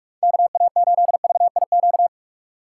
Morse code, heard as a keyed tone, about 46 wpm, 700 Hz. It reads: KA8VIQ
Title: USA Call Signs - 6 Characters - 2x Character Spacing - 45wpm